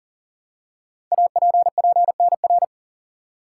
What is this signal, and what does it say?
Morse code, heard as a keyed tone, about 40 wpm, 700 Hz. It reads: APPNR
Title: Code Groups - 5 Letter:Numbers - 40wpm